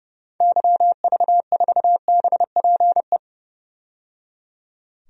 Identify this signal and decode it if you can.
Morse code, heard as a keyed tone, about 30 wpm, 700 Hz. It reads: YV4BPE
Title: DXCC-100 Call Signs - 6 Characters - 30wpm